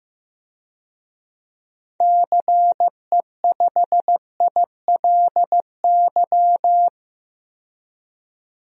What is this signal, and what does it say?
Morse code, heard as a keyed tone, about 14 wpm, 700 Hz. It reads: CE5ILY
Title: DXCC-100 Call Signs - 6 Characters - 15wpm